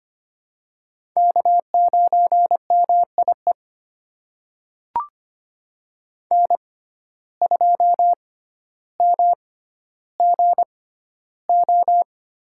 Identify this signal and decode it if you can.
Morse code, heard as a keyed tone, about 25 wpm, spaced as for 13 wpm, 700 Hz. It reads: K9MIE N2MGO
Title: USA Call Signs - 5 Characters - 6x Character Spacing - 25wpm